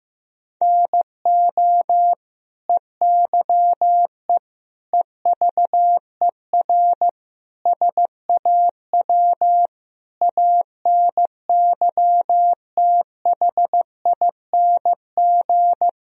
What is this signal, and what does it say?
Morse code, heard as a keyed tone, about 16 wpm, 700 Hz. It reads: NO EYE EVER SAW ANYTHING
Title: Sentences from Top 1000 Words - Review Part 1 - 15wpm